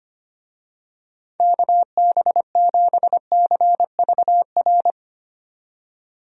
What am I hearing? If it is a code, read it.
Morse code, heard as a keyed tone, about 25 wpm, 700 Hz. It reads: KB7CVR
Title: USA Call Signs - 6 Characters - 25wpm